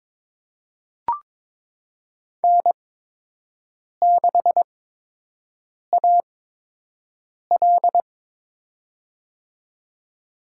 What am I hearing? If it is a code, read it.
Morse code, heard as a keyed tone, about 22 wpm, spaced as for 6 wpm, 700 Hz. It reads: N6AL
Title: USA Call Signs - 8x Character Spacing - 22wpm